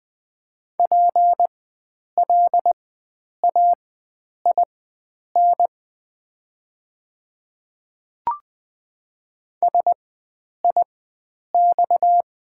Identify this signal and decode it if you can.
Morse code, heard as a keyed tone, about 20 wpm, spaced as for 9 wpm, 700 Hz. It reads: PLAIN SIX
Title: 4x Character Spacing - Top 500 Words - 20wpm